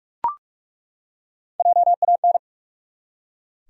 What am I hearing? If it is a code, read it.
Morse code, heard as a keyed tone, about 42 wpm, 700 Hz. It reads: JAN
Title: USA Common Names - Max 3 Letters - 45wpm